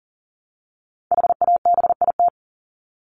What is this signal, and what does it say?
Morse code, heard as a keyed tone, about 40 wpm, 700 Hz. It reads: HABIT
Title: Top 1751-2000 Words 40wpm